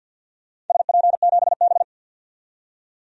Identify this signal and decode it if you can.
Morse code, heard as a keyed tone, about 53 wpm, 700 Hz. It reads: SP7B
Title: DXCC-100 Call Signs - 4 Characters - 50wpm